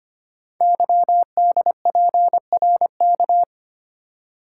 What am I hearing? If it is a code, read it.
Morse code, heard as a keyed tone, about 25 wpm, 700 Hz. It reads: YDPRK
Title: Code Groups - 5 Letters - 25wpm